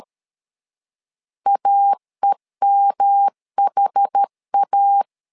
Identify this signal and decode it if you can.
Morse code, heard as a keyed tone, about 13 wpm, 800 Hz. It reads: AEMHA
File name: LEZIONI / 4 Q1S 12wpm 1802